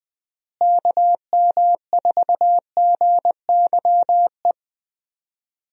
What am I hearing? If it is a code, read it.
Morse code, heard as a keyed tone, about 20 wpm, 700 Hz. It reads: KM4GYE
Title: USA Call Signs - 6 Characters - Repeated 3x - 20wpm